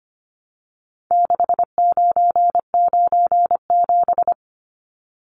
Morse code, heard as a keyed tone, about 25 wpm, 700 Hz. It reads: 6997
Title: Numbers - 4-Digit 25wpm